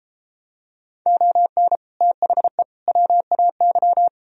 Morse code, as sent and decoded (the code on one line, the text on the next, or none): --- -. / - .... / .-- .- -.--
ON THE WAY